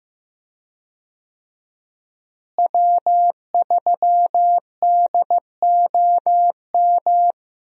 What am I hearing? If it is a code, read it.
Morse code, heard as a keyed tone, about 15 wpm, 700 Hz. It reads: W3DOM
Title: USA Call Signs - 5 Characters - Repeated 2x - 15wpm